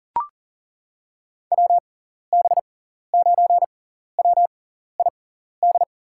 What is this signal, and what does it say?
Morse code, heard as a keyed tone, about 40 wpm, spaced as for 14 wpm, 700 Hz. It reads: WB9WID
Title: USA Call Signs - 6x Character Spacing - 40wpm